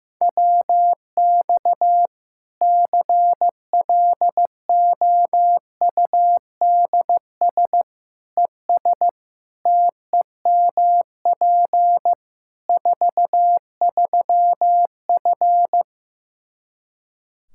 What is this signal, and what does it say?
Morse code, heard as a keyed tone, about 15 wpm, 700 Hz. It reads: WX CLOUDS ES TEMP 43F
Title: K1USN SST QSO Practice 15wpm